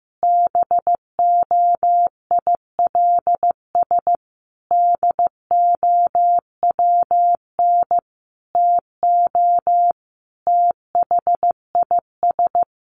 Morse code, heard as a keyed tone, about 15 wpm, 700 Hz. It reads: BOILS DOWN TO THIS
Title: A Sound Of Thunder 15wpm